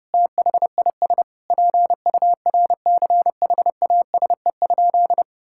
Morse code, heard as a keyed tone, about 30 wpm, 700 Hz. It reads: THIS PURCHASE?
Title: Sentences from Top 1000 Words - Review Part 5 - 30wpm